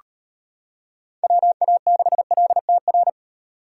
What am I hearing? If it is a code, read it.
Morse code, heard as a keyed tone, about 32 wpm, 700 Hz. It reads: WA6LTR